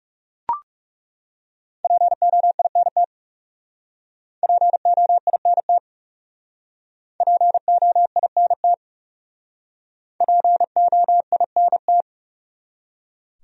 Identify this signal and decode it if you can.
Morse code, heard as a keyed tone, about 40 wpm, 700 Hz. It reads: POINT POINT POINT POINT